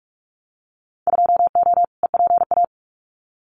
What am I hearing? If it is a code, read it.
Morse code, heard as a keyed tone, about 46 wpm, 690 Hz. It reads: 2O EPA
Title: Winter Field Day Exchanges Encore 45wpm